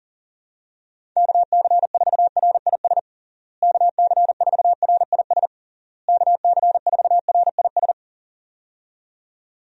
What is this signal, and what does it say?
Morse code, heard as a keyed tone, about 40 wpm, 700 Hz. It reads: KC4RIS KC4RIS KC4RIS
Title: USA Call Signs - 6 Characters - Repeated 3x - 40wpm